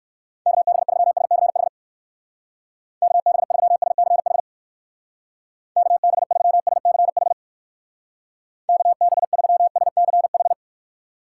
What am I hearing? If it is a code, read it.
Morse code, heard as a keyed tone, about 61 wpm, 700 Hz. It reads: KB3SCH KB3SCH KB3SCH KB3SCH